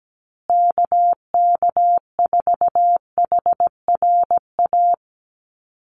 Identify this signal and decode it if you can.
Morse code, heard as a keyed tone, about 17 wpm, 700 Hz. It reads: KK4HRA
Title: USA Call Signs Repeated 2x 17wpm